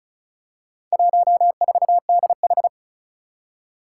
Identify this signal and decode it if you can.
Morse code, heard as a keyed tone, about 36 wpm, 700 Hz. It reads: 14DH
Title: Four Challenging Letter-Number Combinations 35wpm